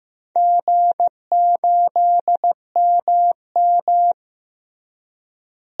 Morse code, heard as a keyed tone, about 15 wpm, 700 Hz. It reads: G8MM
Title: DXCC-100 Call Signs - 4 Characters - Repeated 3x - 15wpm